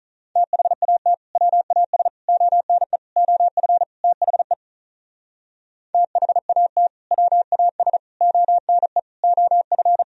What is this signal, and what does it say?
Morse code, heard as a keyed tone, about 42 wpm, 700 Hz. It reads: THAT WAS ONE OF THE THAT WAS ONE OF
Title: Speed-Racing - Sets of 5 Words 35wpm